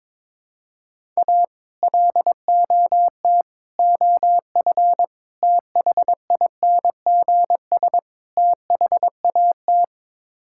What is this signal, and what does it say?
Morse code, heard as a keyed tone, about 22 wpm, 700 Hz. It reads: A LOT OF THINGS THAT
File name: practice / 22wpm / Build A Phrase - 5-Word Phrases - 22wpm